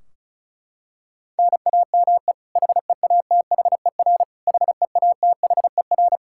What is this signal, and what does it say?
Morse code, heard as a keyed tone, about 34 wpm, 700 Hz. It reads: NAME HEATHER HEATHER
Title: K1USN SST QSO Practice 35wpm